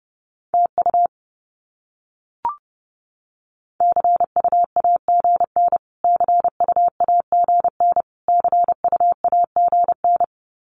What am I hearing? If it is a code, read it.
Morse code, heard as a keyed tone, about 30 wpm, 700 Hz. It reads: TU CUAGN CUAGN CUAGN
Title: Abbreviations Repeated 3x - 30wpm